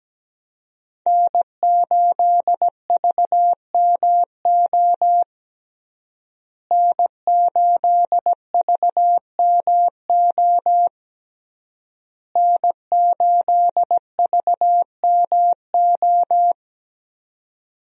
Morse code, heard as a keyed tone, about 17 wpm, 700 Hz. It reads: N8VMO N8VMO N8VMO
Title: USA Call Signs - 5 Characters - Repeated 3x - 17wpm